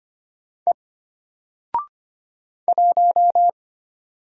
Morse code, heard as a keyed tone, about 25 wpm, 700 Hz. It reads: E 1
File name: Single-Letter-Number-25wpm